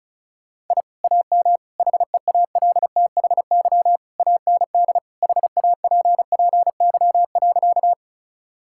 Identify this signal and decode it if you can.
Morse code, heard as a keyed tone, about 36 wpm, 700 Hz. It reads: I AM HEALTHY AND HAPPY.
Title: Sentences from Top 1000 Words - Review Part 5 - 35wpm